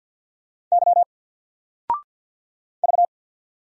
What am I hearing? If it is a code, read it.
Morse code, heard as a keyed tone, about 50 wpm, 700 Hz. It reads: Y V
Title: Single Character (Letter:Number:Punctuation:Prosign) 50wpm